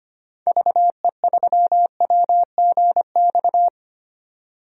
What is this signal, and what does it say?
Morse code, heard as a keyed tone, about 25 wpm, 700 Hz. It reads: VE3WGX
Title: DXCC-100 Call Signs - 6 Characters - 25wpm